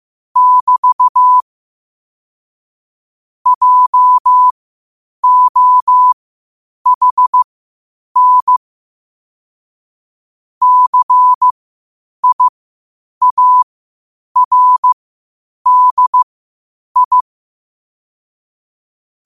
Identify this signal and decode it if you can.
Morse code, heard as a keyed tone, about 15 wpm, spaced as for 8 wpm, 1000 Hz. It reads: <BT> JOHN CIARDI
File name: qotdfiles / 8wpm